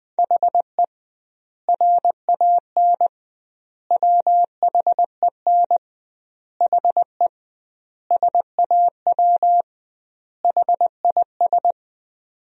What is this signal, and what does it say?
Morse code, heard as a keyed tone, about 20 wpm, 700 Hz. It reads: HE RAN WHEN HE SAW HIS
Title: 2x Word-Spacing - Sentences from Top 500 Words - 20wpm